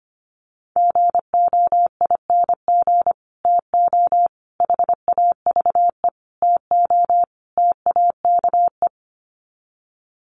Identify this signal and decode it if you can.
Morse code, heard as a keyed tone, about 25 wpm, 700 Hz. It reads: GOING TO HAVE TO TAKE
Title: Sets of 5 Words Encore 25wpm